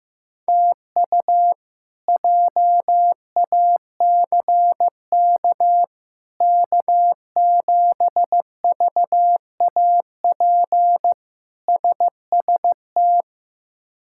Morse code, heard as a keyed tone, about 15 wpm, 700 Hz. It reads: TU JACK K7VAP SST
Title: K1USN SST QSO Practice - 15wpm